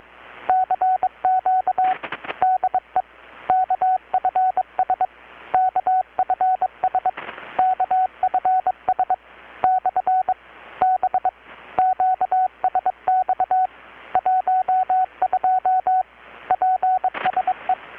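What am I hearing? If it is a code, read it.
Morse code, heard as a keyed tone, about 23 wpm, 750 Hz, with radio noise. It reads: CQ DE KFS KFS KFS / B QSX 12 PSE
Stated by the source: Morse from KPH's Night of Nights XX maritime radiotelegraphy event, received in Chiapas, Mexico about 3,800 km away on a Kenwood R-600 with a grounded 10 m long-wire antenna.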